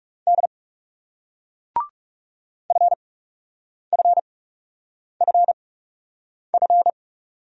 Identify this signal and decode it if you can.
Morse code, heard as a keyed tone, about 37 wpm, spaced as for 8 wpm, 700 Hz. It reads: D FFFF
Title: Speed-Racing Single Letters 30wpm